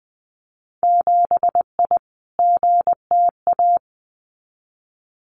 Morse code, heard as a keyed tone, about 20 wpm, 700 Hz. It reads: 7I GTA
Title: Winter Field Day Exchanges Encore 20wpm